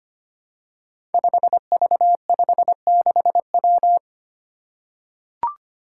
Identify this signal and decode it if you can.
Morse code, heard as a keyed tone, about 25 wpm, 700 Hz. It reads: 5V56W
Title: Five Challenging Letter-Number Combinations 25wpm